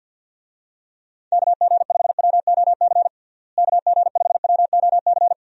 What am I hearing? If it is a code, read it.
Morse code, heard as a keyed tone, about 50 wpm, 700 Hz. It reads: KG5WOC KG5WOC
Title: USA Call Signs - 6 Characters - Repeated 3x - 50wpm